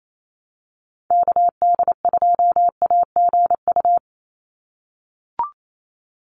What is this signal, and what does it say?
Morse code, heard as a keyed tone, about 28 wpm, 700 Hz. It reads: KD2AGU
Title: USA Call Signs 28wpm